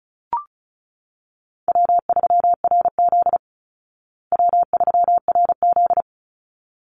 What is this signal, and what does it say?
Morse code, heard as a keyed tone, about 34 wpm, 700 Hz. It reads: W3RZ W3RZ
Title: Easy Call Signs Repeated 2x - 35wpm